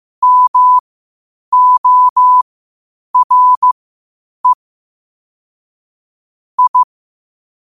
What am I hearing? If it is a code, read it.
Morse code, heard as a keyed tone, about 15 wpm, spaced as for 8 wpm, 1000 Hz. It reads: MORE I
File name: qotdfiles / 8wpm